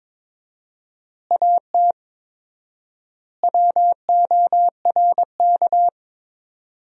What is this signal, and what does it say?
Morse code, heard as a keyed tone, about 21 wpm, 700 Hz. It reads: AT WORK
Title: 4x Word-Spacing - Sentences from Top 200 Words - 22wpm